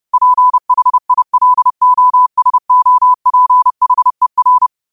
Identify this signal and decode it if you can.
Morse code, heard as a keyed tone, about 30 wpm, 1000 Hz. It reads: PHILOSOPHER